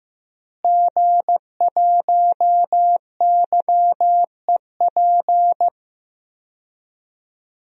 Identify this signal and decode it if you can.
Morse code, heard as a keyed tone, about 15 wpm, 700 Hz. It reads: G1YEP
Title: DXCC-100 Call Signs - 5 Characters - Repeated 2x - 15wpm